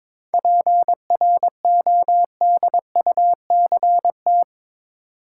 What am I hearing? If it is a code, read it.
Morse code, heard as a keyed tone, about 22 wpm, 700 Hz. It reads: PRODUCT
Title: Sentences from Top 1400 Words - New Word Drill Alt - 22wpm